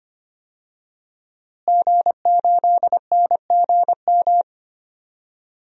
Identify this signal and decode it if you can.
Morse code, heard as a keyed tone, about 25 wpm, 700 Hz. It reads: G8NGM
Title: DXCC-100 Call Signs - 5 Characters - Repeated 3x - 25wpm